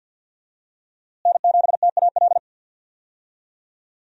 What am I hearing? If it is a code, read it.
Morse code, heard as a keyed tone, about 51 wpm, 700 Hz. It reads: N7TAL
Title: USA Call Signs - 5 Characters - Repeated 2x - 50wpm